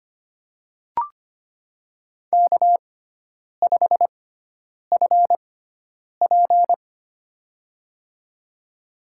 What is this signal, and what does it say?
Morse code, heard as a keyed tone, about 25 wpm, spaced as for 9 wpm, 700 Hz. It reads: K5FP
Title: USA Call Signs - 4 Characters - 6x Character Spacing - 25wpm